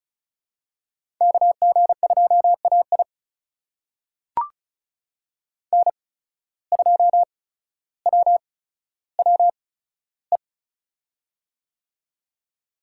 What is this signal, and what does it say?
Morse code, heard as a keyed tone, about 36 wpm, spaced as for 15 wpm, 690 Hz. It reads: KG2AI N2WWE
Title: USA Call Signs - 5 Characters - 8x Character Spacing - 35wpm